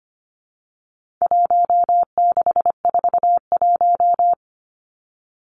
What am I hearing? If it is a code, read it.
Morse code, heard as a keyed tone, about 25 wpm, 700 Hz. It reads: 1641
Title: Numbers - Contest Style - 4-Digit 25wpm